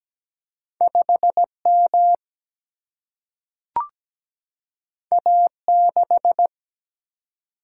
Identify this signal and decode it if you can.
Morse code, heard as a keyed tone, about 17 wpm, 700 Hz. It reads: 5M A6
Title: Single-Letter Single-Number 17wpm